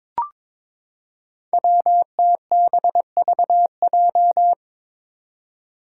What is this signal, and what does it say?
Morse code, heard as a keyed tone, about 22 wpm, 700 Hz. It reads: WTBVJ
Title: Five Challenging Letter-Number Combinations 22wpm